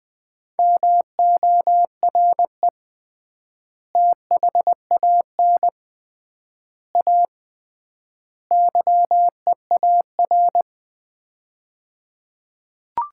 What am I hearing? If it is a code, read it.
Morse code, heard as a keyed tone, about 20 wpm, 700 Hz. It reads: MORE THAN A YEAR
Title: 3x Word-Spacing - Sets of 4 Words - 20wpm